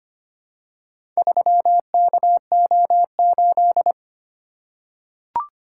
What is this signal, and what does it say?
Morse code, heard as a keyed tone, about 25 wpm, 700 Hz. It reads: 3KO8
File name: practice / 25wpm / Four Letter-Number Combinations - 25wpm